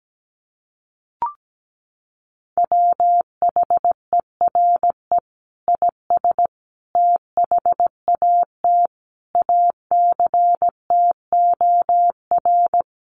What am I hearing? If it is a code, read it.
Morse code, heard as a keyed tone, about 17 wpm, 700 Hz. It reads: WHERE IS THAT ACTOR
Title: Sentences from Top 1500 Words 17wpm